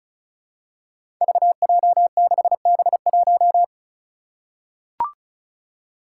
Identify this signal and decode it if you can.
Morse code, heard as a keyed tone, about 36 wpm, 700 Hz. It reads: VJ6B1
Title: Five Challenging Letter-Number Combinations 35wpm